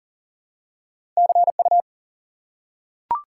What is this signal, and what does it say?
Morse code, heard as a keyed tone, about 38 wpm, 700 Hz. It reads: CU
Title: Top 180 QSO Elements 40wpm